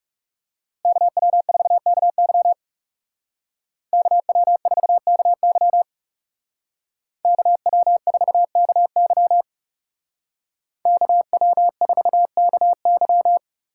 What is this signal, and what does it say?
Morse code, heard as a keyed tone, about 40 wpm, 700 Hz. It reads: KW4KY KW4KY KW4KY KW4KY